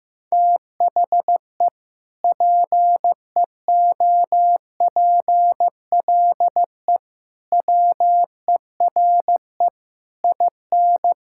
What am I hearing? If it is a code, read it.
Morse code, heard as a keyed tone, about 15 wpm, 700 Hz. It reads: THE PEOPLE WERE IN
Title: Sentences from Top 1600 Words 15wpm